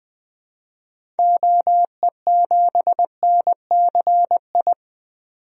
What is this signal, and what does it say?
Morse code, heard as a keyed tone, about 20 wpm, 700 Hz. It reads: OE7NCI
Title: DXCC-100 Call Signs - 6 Characters - 20wpm